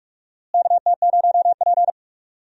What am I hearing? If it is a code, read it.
Morse code, heard as a keyed tone, about 42 wpm, 700 Hz. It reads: KT0P
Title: USA Call Signs - 4 Characters - Repeated 3x - 45wpm